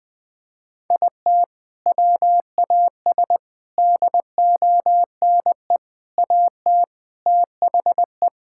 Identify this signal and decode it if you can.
Morse code, heard as a keyed tone, about 20 wpm, 700 Hz. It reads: IT WAS DONE AT THE
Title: Sentences with 4-Word Phrases - Spoken Beginning - 20wpm